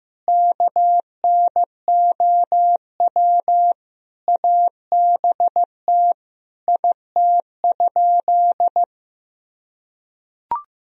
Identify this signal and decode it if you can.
Morse code, heard as a keyed tone, about 15 wpm, 700 Hz. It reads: KNOW ABT IT?
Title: Sentences from Top 100 Words and Abbreviations 15wpm